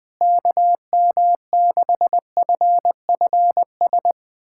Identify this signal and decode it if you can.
Morse code, heard as a keyed tone, about 20 wpm, 700 Hz. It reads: KM6FFS
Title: USA Callsign DE Callsign 20wpm